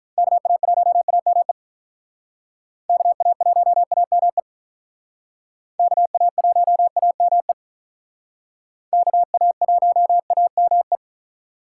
Morse code, heard as a keyed tone, about 44 wpm, 700 Hz. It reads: KA1AME KA1AME KA1AME KA1AME
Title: Speed Racing - USA 2x3 Call Signs 35wpm